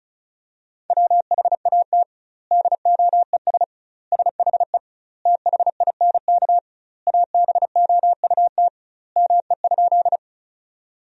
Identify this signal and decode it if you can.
Morse code, heard as a keyed tone, about 36 wpm, 700 Hz. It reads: WHAT DOES SHE THINK ABOUT ME?
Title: 2x Word-Spacing - Sentences from Top 200 Words - 35wpm